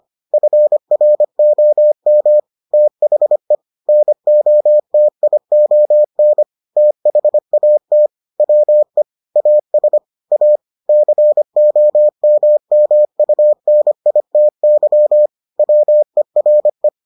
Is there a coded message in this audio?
Yes: Morse code from 0.3 to 16.9 s, heard as a keyed tone, about 25 wpm, 600 Hz. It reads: FROM THE NOTION THAT WE AS A COMMUNITY WERE